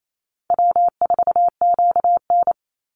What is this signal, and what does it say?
Morse code, heard as a keyed tone, about 28 wpm, 700 Hz. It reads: W4QN
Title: Easy Call Signs - 28wpm